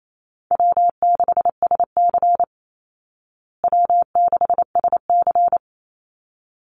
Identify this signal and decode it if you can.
Morse code, heard as a keyed tone, about 28 wpm, 700 Hz. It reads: W6SC W6SC
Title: Easy Call Signs Repeated 2x - 28wpm